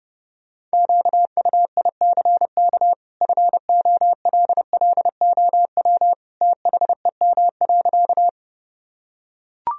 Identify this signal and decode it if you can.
Morse code, heard as a keyed tone, about 30 wpm, 700 Hz. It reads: QUICK FOLLOW THEM.
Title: Sentences from Top 1000 Words - Review Part 1 - 30wpm